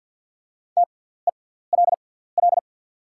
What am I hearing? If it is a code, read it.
Morse code, heard as a keyed tone, about 48 wpm, spaced as for 17 wpm, 700 Hz. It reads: TELL
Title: 6x Character Spacing - Top 300 Words - 50wpm